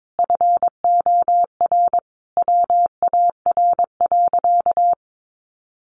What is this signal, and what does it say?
Morse code, heard as a keyed tone, about 22 wpm, 700 Hz. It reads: FOR WAR.
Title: Sentences from Top 1300 Words 22wpm